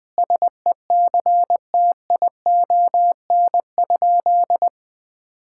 Morse code, heard as a keyed tone, about 19 wpm, 700 Hz. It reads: SECTION?
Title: Sentences from Top 1000 Words - Review Part 5 - 20wpm